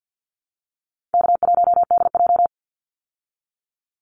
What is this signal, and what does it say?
Morse code, heard as a keyed tone, about 50 wpm, 700 Hz. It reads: K1DJ
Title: Easy Call Signs Repeated 2x - 50wpm